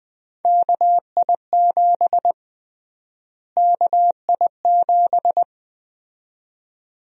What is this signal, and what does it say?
Morse code, heard as a keyed tone, about 20 wpm, 700 Hz. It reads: KI7 KI7
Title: USA Call Sign Prefixes - Repeated 2x - 20wpm